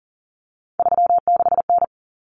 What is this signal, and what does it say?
Morse code, heard as a keyed tone, about 40 wpm, 700 Hz. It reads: 36N
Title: Common Cut Numbers - 3-Digit 40wpm